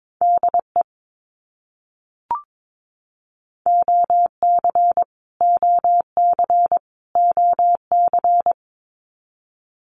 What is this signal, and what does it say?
Morse code, heard as a keyed tone, about 22 wpm, 700 Hz. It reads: DE OC OC OC
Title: Abbreviations Repeated 3x - 22wpm